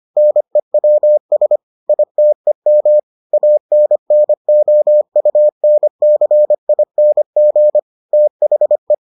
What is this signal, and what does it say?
Morse code, heard as a keyed tone, about 25 wpm, 600 Hz. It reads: NEWS ITEM ANNOUNCING THE